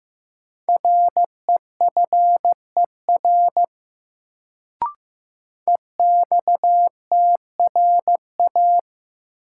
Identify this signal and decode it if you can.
Morse code, heard as a keyed tone, about 15 wpm, 700 Hz. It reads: REFER EXTRA
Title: Sentences from Top 1000 Words - New Word Drill - 15wpm